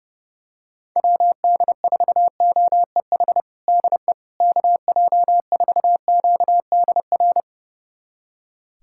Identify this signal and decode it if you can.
Morse code, heard as a keyed tone, about 29 wpm, 700 Hz. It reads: WD4OEH DE KJ4QDR
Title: USA Callsign DE Callsign - 6 Characters - 30wpm